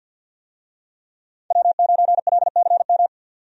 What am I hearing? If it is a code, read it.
Morse code, heard as a keyed tone, about 52 wpm, 700 Hz. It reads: W9LCM
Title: USA Call Signs - 5 Characters - 50wpm